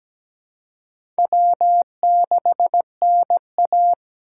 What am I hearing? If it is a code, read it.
Morse code, heard as a keyed tone, about 17 wpm, 700 Hz. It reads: W6NA
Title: USA Call Signs - 4 Characters - 2x Character Spacing - 17wpm